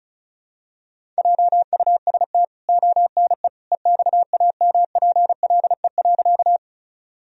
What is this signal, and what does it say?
Morse code, heard as a keyed tone, about 34 wpm, 700 Hz. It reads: JUST ONE EXAMPLE.
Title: Sentences from Top 1000 Words - Review Part 1 - 35wpm